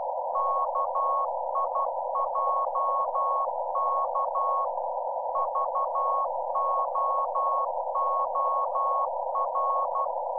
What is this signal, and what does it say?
Morse code, heard as a keyed tone, about 12 wpm, 1100 Hz. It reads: KIJK VOOR
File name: Z PI4VRZ / PI4VRZ 14-jun-2025